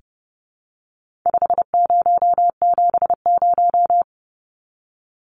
Morse code, heard as a keyed tone, about 30 wpm, 700 Hz. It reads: 5070
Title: Numbers - 4-Digit 30wpm